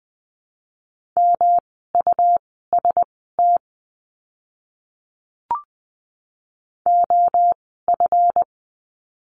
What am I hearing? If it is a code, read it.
Morse code, heard as a keyed tone, about 19 wpm, spaced as for 14 wpm, 700 Hz. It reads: MUST OF
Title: 237 Lesson 38f - Letter q - Words 20wpm